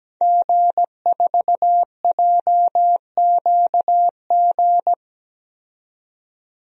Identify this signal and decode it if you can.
Morse code, heard as a keyed tone, about 17 wpm, 700 Hz. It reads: G4JQG
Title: DXCC-100 Call Signs - 5 Characters - 17wpm